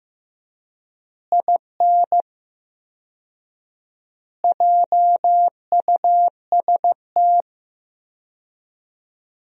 Morse code, heard as a keyed tone, about 15 wpm, 700 Hz. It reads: IN JUST
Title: 4x Word-Spacing - Sentences from Top 300 Words - 15wpm